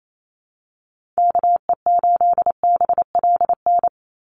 Morse code, heard as a keyed tone, about 28 wpm, 700 Hz. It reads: KE8BLN
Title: USA Call Signs 28wpm